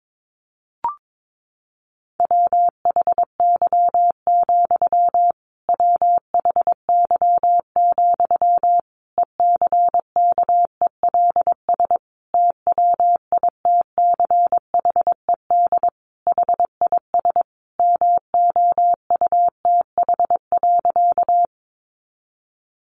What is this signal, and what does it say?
Morse code, heard as a keyed tone, about 22 wpm, 700 Hz. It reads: WHY, WHY, ECKELS TWITCHED HIS MOUTH.
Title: A Sound Of Thunder 22wpm